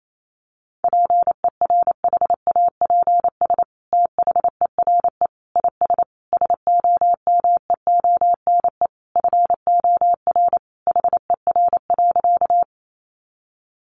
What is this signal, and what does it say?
Morse code, heard as a keyed tone, about 28 wpm, 700 Hz. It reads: PERHAPS THERE IS SOMEONE FOR HER.
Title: Sentences from Top 600 Words 28wpm